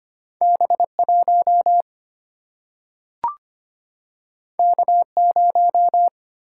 Morse code, heard as a keyed tone, about 25 wpm, 700 Hz. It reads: B 1 K 0
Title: Single-Letter Single-Number 25wpm